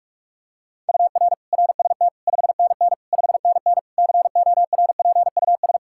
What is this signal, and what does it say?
Morse code, heard as a keyed tone, about 46 wpm, 700 Hz. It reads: UR RST 5NN 5NN CORPUS
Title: K1USN SST QSO Practice 45wpm